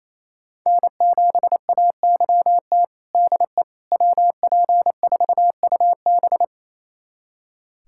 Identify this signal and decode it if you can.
Morse code, heard as a keyed tone, about 28 wpm, 700 Hz. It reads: N7AYT DE WP4UB
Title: USA Callsign DE Callsign - 5 Characters - 28wpm